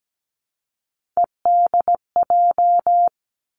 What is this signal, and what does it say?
Morse code, heard as a keyed tone, about 17 wpm, 700 Hz. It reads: EDJ
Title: Three Challenging Letter-Number Combinations 17wpm